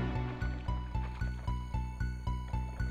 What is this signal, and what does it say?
There is no code.